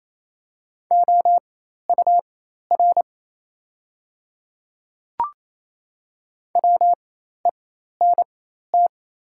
Morse code, heard as a keyed tone, about 27 wpm, spaced as for 13 wpm, 690 Hz. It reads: OUR WENT
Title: 4x Character Spacing - Top 200 Words - 28wpm